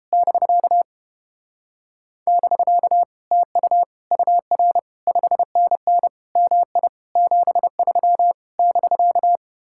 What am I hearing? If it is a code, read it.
Morse code, heard as a keyed tone, about 29 wpm, 700 Hz. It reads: <BK> <BK> TU UR 5NN MI 73 <BK>